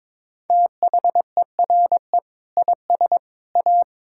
Morse code, heard as a keyed tone, about 22 wpm, 700 Hz. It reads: THERE IS A
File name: practice / 22wpm / Sentences with 2-Word Phrases - 22wpm